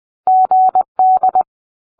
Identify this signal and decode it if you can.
Morse code, heard as a keyed tone, about 20 wpm, 750 Hz. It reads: GD